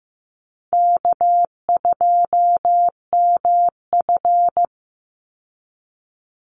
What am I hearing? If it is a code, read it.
Morse code, heard as a keyed tone, about 15 wpm, 700 Hz. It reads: K2MF
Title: Easy Call Signs Repeated 2x - 15wpm